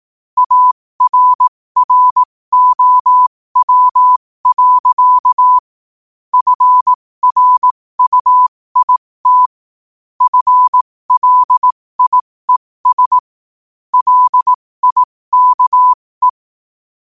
Morse code, heard as a keyed tone, about 18 wpm, spaced as for 15 wpm, 1000 Hz. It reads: ARROW. FRUIT FLIES LIKE
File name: qotdfiles / 15wpm